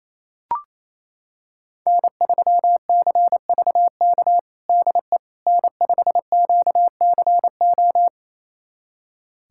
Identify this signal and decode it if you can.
Morse code, heard as a keyed tone, about 28 wpm, 700 Hz. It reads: N3CVK DE N5QCO
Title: USA Callsign DE Callsign - 5 Characters - 28wpm